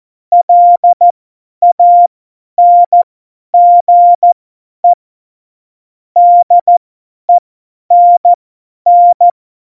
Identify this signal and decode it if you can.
Morse code, heard as a keyed tone, about 14 wpm, spaced as for 10 wpm, 700 Hz. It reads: LANGE DENN
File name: Morsekurs / Witze / Witz 20 March 2018, 50 F70